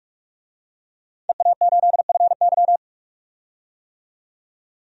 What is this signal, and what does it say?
Morse code, heard as a keyed tone, about 44 wpm, 700 Hz. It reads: EA8FY
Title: DXCC-100 Call Signs - 5 Characters - 45wpm